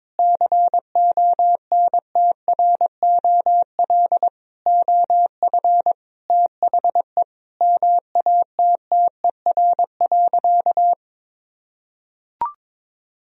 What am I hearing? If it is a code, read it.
Morse code, heard as a keyed tone, about 21 wpm, 690 Hz. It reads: CONTROL OF THE MATTER.
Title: Sentences from Top 1000 Words - Review Part 3 - 22wpm